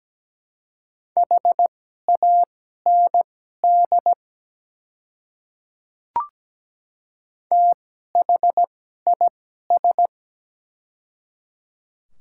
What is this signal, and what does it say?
Morse code, heard as a keyed tone, about 17 wpm, spaced as for 12 wpm, 700 Hz. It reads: HAND THIS
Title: 2x Character Spacing - Top 200 Words - 17wpm